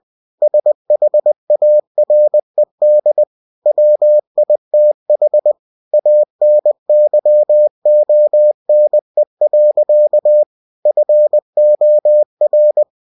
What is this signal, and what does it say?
Morse code, heard as a keyed tone, about 20 wpm, 600 Hz. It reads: SHARED WITH ANYONE. FOR
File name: cw-books / winnie-the-pooh / winnie-20wpm-page-0122